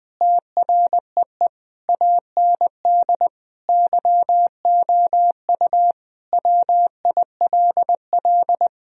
Morse code, heard as a keyed tone, about 20 wpm, 700 Hz. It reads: TREE AND YOU WILL
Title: Sentences from Top 1000 Words - Review Part 1 - 20wpm